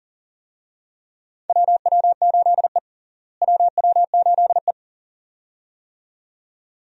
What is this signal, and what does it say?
Morse code, heard as a keyed tone, about 40 wpm, 700 Hz. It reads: WW8E WW8E
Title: USA Call Signs - 4 Characters - Repeated 2x - 40wpm